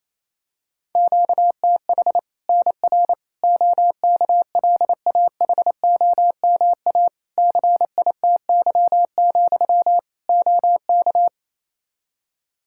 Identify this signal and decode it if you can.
Morse code, heard as a keyed tone, about 28 wpm, 700 Hz. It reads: QTH NR OKLAHOMA CITY, OK
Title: Practice QSO - Exchange 1 - Beginning - 28wpm